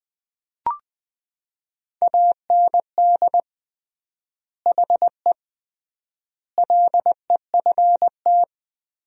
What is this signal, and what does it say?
Morse code, heard as a keyed tone, about 21 wpm, 700 Hz. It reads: AND HE LEFT
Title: 3x Word-Spacing - Sentences from Top 300 Words - 20wpm